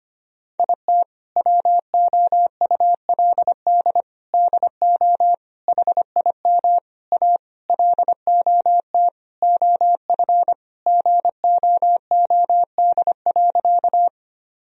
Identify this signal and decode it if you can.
Morse code, heard as a keyed tone, about 25 wpm, 700 Hz. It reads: IT WOULD DO HIM A LOT OF GOOD.
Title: Sentences from Top 1000 Words - Review Part 3 - 25wpm